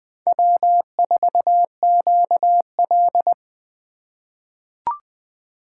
Morse code, heard as a keyed tone, about 20 wpm, 700 Hz. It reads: W4QL
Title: USA Call Signs - 4 Characters - 6x Character Spacing - 20wpm